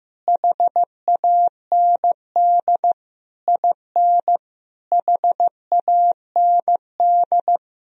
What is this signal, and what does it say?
Morse code, heard as a keyed tone, about 15 wpm, 700 Hz. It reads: HAND IN HAND
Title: Sentences from Top 1000 Words - Review Part 4 - 15wpm